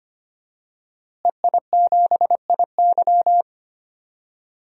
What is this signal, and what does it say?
Morse code, heard as a keyed tone, about 25 wpm, 700 Hz. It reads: EI7IY
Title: DXCC-100 Call Signs - 5 Characters - Repeated 3x - 25wpm